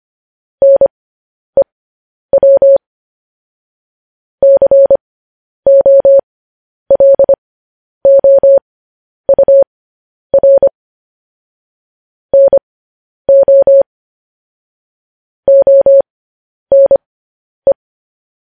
Morse code, heard as a keyed tone, about 25 wpm, spaced as for 10 wpm, 550 Hz. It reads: NEW COLOUR NO ONE
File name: podcast / News headlines-1x-25x10